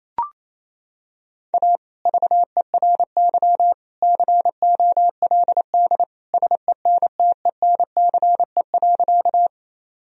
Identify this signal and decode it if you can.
Morse code, heard as a keyed tone, about 27 wpm, 700 Hz. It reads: A VERY COLD SENTENCE.
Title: Sentences from Top 1000 Words - Review Part 1 - 28wpm